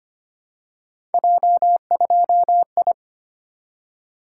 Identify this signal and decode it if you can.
Morse code, heard as a keyed tone, about 25 wpm, 700 Hz. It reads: J2I
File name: Three Letter-Number Combinations - 25wpm